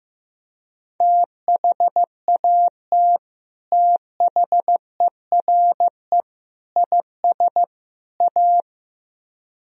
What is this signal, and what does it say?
Morse code, heard as a keyed tone, about 17 wpm, 700 Hz. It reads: THAT THERE IS A
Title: Speed-Racing - Sets of 4 Words 15wpm